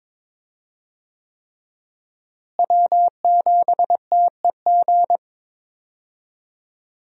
Morse code, heard as a keyed tone, about 21 wpm, 700 Hz. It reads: W7TEG
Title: USA Call Signs - 5 Characters - Repeated 3x - 22wpm